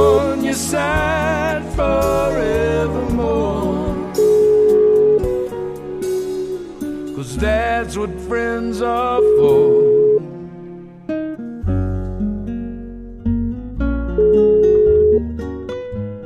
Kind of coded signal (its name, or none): telephone call progress tone